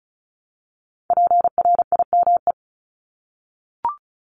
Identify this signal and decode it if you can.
Morse code, heard as a keyed tone, about 36 wpm, 700 Hz. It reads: PRIME
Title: Multi-letter Phonemes Repeated 3x 35wpm